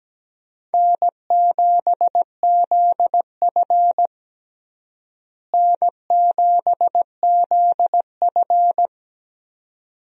Morse code, heard as a keyed tone, about 17 wpm, 700 Hz. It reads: N7ZF N7ZF
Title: USA Call Signs - 4 Characters - Repeated 2x - 17wpm